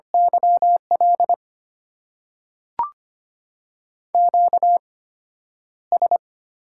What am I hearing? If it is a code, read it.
Morse code, heard as a keyed tone, about 25 wpm, spaced as for 11 wpm, 700 Hz. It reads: YL QS
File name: Top 90 QSO Elements - 8x Character Spacing - 25wpm